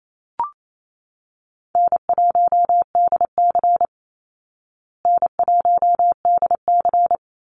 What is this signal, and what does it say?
Morse code, heard as a keyed tone, about 28 wpm, 700 Hz. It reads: N1DC N1DC
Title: Easy Call Signs Repeated 2x - 28wpm